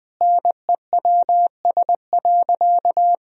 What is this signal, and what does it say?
Morse code, heard as a keyed tone, about 21 wpm, 700 Hz. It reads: NEWS.
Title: Sentences from Top 1000 Words - Review Part 4 - 20wpm